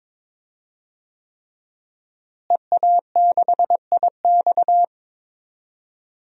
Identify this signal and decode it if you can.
Morse code, heard as a keyed tone, about 22 wpm, 700 Hz. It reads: EA6IX
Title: DXCC-100 Call Signs - 5 Characters - Repeated 3x - 22wpm